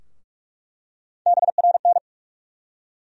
Morse code, heard as a keyed tone, about 46 wpm, 700 Hz. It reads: BRN
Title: USA Call Sign Postfixes - 3 Letters 45wpm